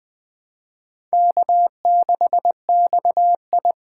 Morse code, heard as a keyed tone, about 20 wpm, 700 Hz. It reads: K6XI
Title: Speed Racing - USA Call Signs 17wpm